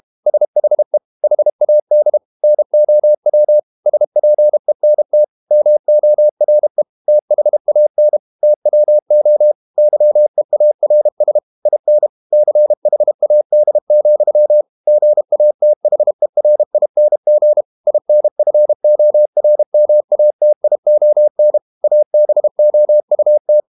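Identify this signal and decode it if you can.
Morse code, heard as a keyed tone, about 32 wpm, 600 Hz. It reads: SHE HAD NOW SPENT MORE THAN TWO YEARS IN CHAD, GATHERING INFORMATION ABOUT
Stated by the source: Morse code practice at 32 wpm from CW med Gnister, reading Ken Follett's novel Never in English.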